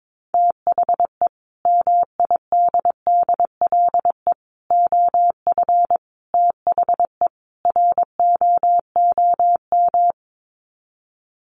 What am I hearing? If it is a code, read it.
Morse code, heard as a keyed tone, about 22 wpm, 700 Hz. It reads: THE MIDDLE OF THE ROOM
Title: Sets of 5 Words Encore2 22wpm